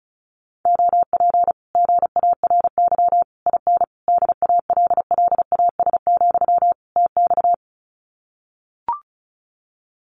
Morse code, heard as a keyed tone, about 34 wpm, 700 Hz. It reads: OP GARY IN DALLAS, TX
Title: Common QSO Phrases Encore 35wpm